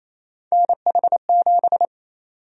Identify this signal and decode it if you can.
Morse code, heard as a keyed tone, about 27 wpm, 700 Hz. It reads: NH7
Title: USA Call Sign Prefixes - Repeated 2x - 28wpm